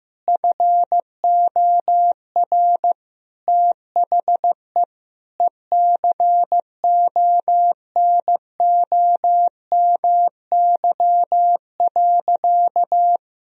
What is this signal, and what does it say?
Morse code, heard as a keyed tone, about 15 wpm, 700 Hz. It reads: FOR THE ECONOMY.
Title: Sentences from Top 1000 Words - Review Part 4 - 15wpm